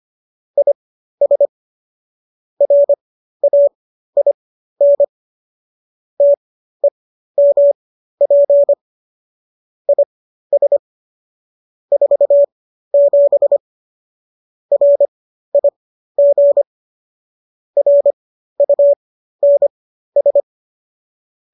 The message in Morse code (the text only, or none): IS RAIN TEMP IS 47 RIG RUNS